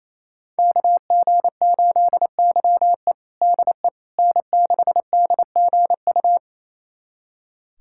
Morse code, heard as a keyed tone, about 28 wpm, 700 Hz. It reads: KG8YE DE N6DGU
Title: USA Callsign DE Callsign - 5 Characters - 28wpm